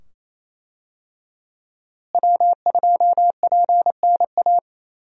Morse code, heard as a keyed tone, about 27 wpm, 700 Hz. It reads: W2PNA
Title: USA Call Signs - 5 Characters - Repeated 2x - 28wpm